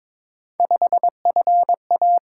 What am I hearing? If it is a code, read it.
Morse code, heard as a keyed tone, about 21 wpm, 700 Hz. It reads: 5FA
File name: practice / 22wpm / Three Letter-Number Combinations - 22wpm